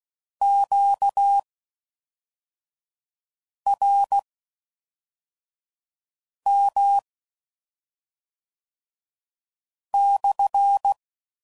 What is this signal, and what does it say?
Morse code, heard as a keyed tone, about 16 wpm, spaced as for 3 wpm, 800 Hz. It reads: QRM/